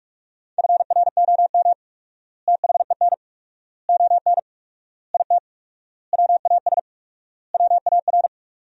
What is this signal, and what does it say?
Morse code, heard as a keyed tone, about 46 wpm, 700 Hz. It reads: FROM THEN ON IT WAS WAR
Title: 4x Word-Spacing - Sentences from Top 400 Words - 45wpm